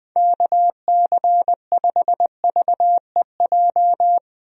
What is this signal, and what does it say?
Morse code, heard as a keyed tone, about 20 wpm, 700 Hz. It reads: KC5VEJ
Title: USA Call Signs - 6 Characters - 20wpm